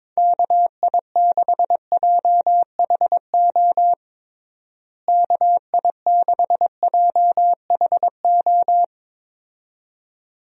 Morse code, heard as a keyed tone, about 21 wpm, 700 Hz. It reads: KI6JHO KI6JHO
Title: USA Call Signs - 6 Characters - Repeated 2x - 22wpm